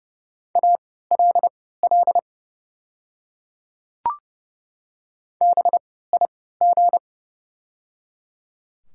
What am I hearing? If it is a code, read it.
Morse code, heard as a keyed tone, about 29 wpm, spaced as for 17 wpm, 690 Hz. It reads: ALL BIG
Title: 3x Character Spacing - Top 200 Words - 30wpm